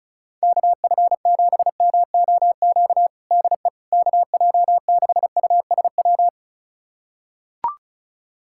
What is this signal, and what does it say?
Morse code, heard as a keyed tone, about 36 wpm, 700 Hz. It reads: KF7MOQ DE KJ6USW
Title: USA Callsign DE Callsign - 6 Characters - 35wpm